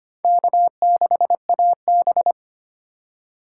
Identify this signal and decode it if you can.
Morse code, heard as a keyed tone, about 31 wpm, 700 Hz. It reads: K6AB
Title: Speed Racing - USA Call Signs 25wpm